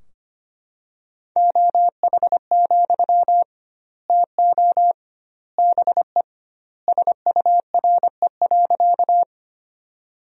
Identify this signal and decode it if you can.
Morse code, heard as a keyed tone, about 25 wpm, 700 Hz. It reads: OH, TO BE SURE.
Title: 2x Word-Spacing - Sentences from Top 500 Words - 25wpm